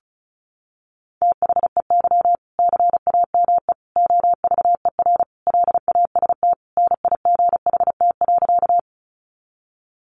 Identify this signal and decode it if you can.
Morse code, heard as a keyed tone, about 34 wpm, 700 Hz. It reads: THEY CAME OVER LAST NIGHT.
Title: Sentences from Top 300 Words 35wpm